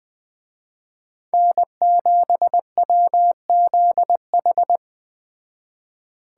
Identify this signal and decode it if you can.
Morse code, heard as a keyed tone, about 21 wpm, 700 Hz. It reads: N7WZH
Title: Speed Racing - USA 1x3 Call Signs 17wpm